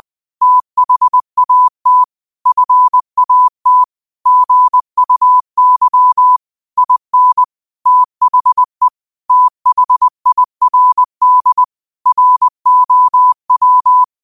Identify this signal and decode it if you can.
Morse code, heard as a keyed tone, about 20 wpm, 1000 Hz. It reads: THAT FAT GUY IN THE THIRD ROW
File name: 20wpm